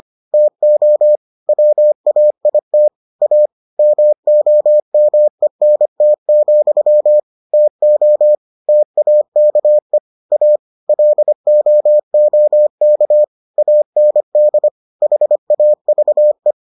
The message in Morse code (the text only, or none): TO WAIT A MOMENT, TO TAKE A LOOK AND HAVE